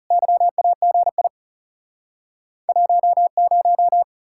Morse code, heard as a keyed tone, about 44 wpm, 700 Hz. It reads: YAGI 10